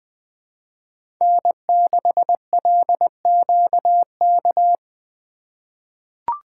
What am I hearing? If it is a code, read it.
Morse code, heard as a keyed tone, about 20 wpm, 700 Hz. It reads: N6LQK
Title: USA Call Signs - 5 Characters - 2x Character Spacing - 20wpm